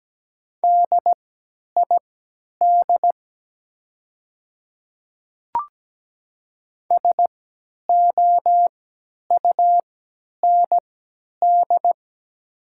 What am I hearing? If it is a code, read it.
Morse code, heard as a keyed tone, about 17 wpm, spaced as for 10 wpm, 700 Hz. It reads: DID SOUND
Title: 3x Character Spacing - Top 100 Words - 17wpm